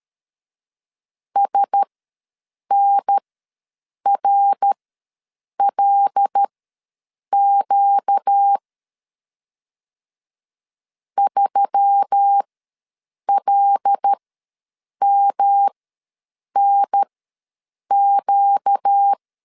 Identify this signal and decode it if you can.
Morse code, heard as a keyed tone, about 13 wpm, spaced as for 7 wpm, 800 Hz. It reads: SNRLQ 3LMNQ
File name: LEZIONI / 5 LBVC / 5 L ws 12wpm 2958